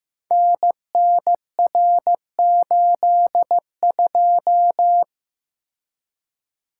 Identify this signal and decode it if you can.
Morse code, heard as a keyed tone, about 15 wpm, 700 Hz. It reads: NNR82
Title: Code Groups - 5 Letter:Numbers - 15wpm